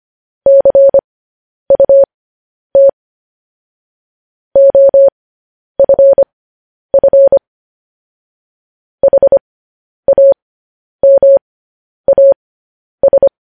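Morse code, heard as a keyed tone, about 25 wpm, spaced as for 10 wpm, 550 Hz. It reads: CUT OFF HAMAS
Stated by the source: Morse code news headlines at 10 wpm.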